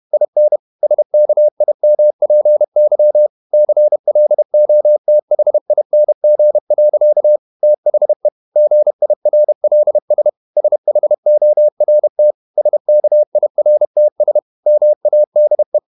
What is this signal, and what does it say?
Morse code, heard as a keyed tone, about 31 wpm, 600 Hz. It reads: IN SKIMPY CLOTHING. THE GIRLS SHORT SKIRTS MADE
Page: Never 31wpm | CW med Gnister